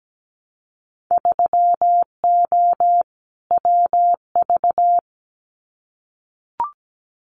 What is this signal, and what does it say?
Morse code, heard as a keyed tone, about 17 wpm, 700 Hz. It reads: 3O WV
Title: Winter Field Day Exchanges 17wpm